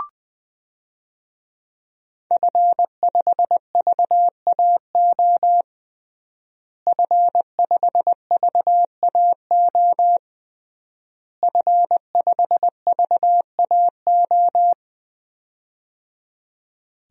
Morse code, heard as a keyed tone, about 20 wpm, 700 Hz. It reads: F5VAO F5VAO F5VAO
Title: DXCC-100 Call Signs - 5 Characters - Repeated 3x - 20wpm